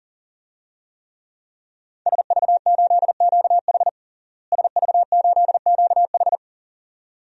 Morse code, heard as a keyed tone, about 40 wpm, 700 Hz. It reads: SV8QH SV8QH
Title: DXCC-100 Call Signs - 5 Characters - Repeated 2x - 40wpm